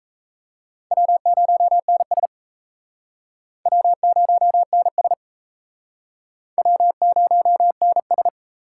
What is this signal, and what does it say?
Morse code, heard as a keyed tone, about 38 wpm, 700 Hz. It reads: W0NS W0NS W0NS